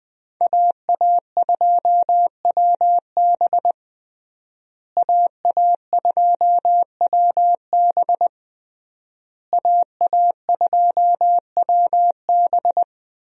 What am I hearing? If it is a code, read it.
Morse code, heard as a keyed tone, about 20 wpm, 700 Hz. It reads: AA2WB AA2WB AA2WB
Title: USA Call Signs - 5 Characters - Repeated 3x - 20wpm